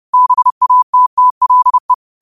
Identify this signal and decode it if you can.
Morse code, heard as a keyed tone, about 30 wpm, 1000 Hz. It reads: BATTLE
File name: qotdfiles / 30wpm